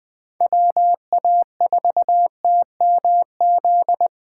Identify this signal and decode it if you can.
Morse code, heard as a keyed tone, about 20 wpm, 700 Hz. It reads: WA4TMZ
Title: USA Call Signs - 6 Characters - 3x Character Spacing - 20wpm